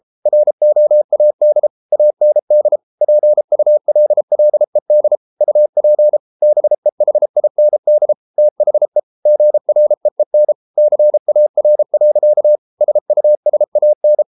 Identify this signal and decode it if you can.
Morse code, heard as a keyed tone, about 33 wpm, 600 Hz. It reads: ROAD AND PULLED UP BEHIND THE GREEN CAR. SUSAN